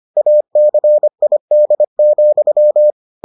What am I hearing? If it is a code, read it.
Morse code, heard as a keyed tone, about 25 wpm, 600 Hz. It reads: ACID,